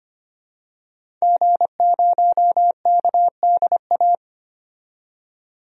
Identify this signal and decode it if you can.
Morse code, heard as a keyed tone, about 25 wpm, 700 Hz. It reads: G0KDA
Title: DXCC-100 Call Signs - 5 Characters - 25wpm